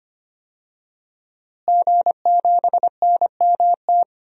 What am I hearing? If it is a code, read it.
Morse code, heard as a keyed tone, about 25 wpm, 700 Hz. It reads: G7NMT
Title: DXCC-100 Call Signs - 5 Characters - Repeated 3x - 25wpm